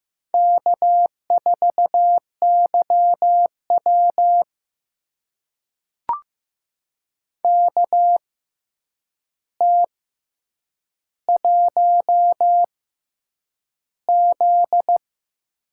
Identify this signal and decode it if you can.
Morse code, heard as a keyed tone, about 15 wpm, spaced as for 8 wpm, 700 Hz. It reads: K4YW KT1Z
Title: USA Call Signs - 4 Characters - 6x Character Spacing - 15wpm